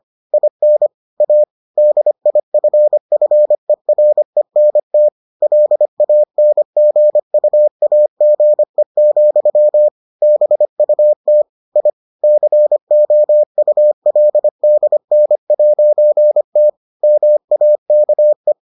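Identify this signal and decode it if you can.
Morse code, heard as a keyed tone, about 25 wpm, 600 Hz. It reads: IN A DIFFERENT LANGUAGE, BUT I COULDN'T MAKE
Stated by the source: Morse edition of the Foundations of Amateur Radio episodes.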